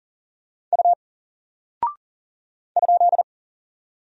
Morse code, heard as a keyed tone, about 40 wpm, 700 Hz. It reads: U ?
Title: Single Character (Letter:Number:Punctuation:Prosign) 40wpm